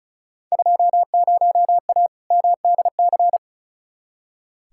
Morse code, heard as a keyed tone, about 36 wpm, 690 Hz. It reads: 20A MDC
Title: Field Day Exchanges Encore - 35wpm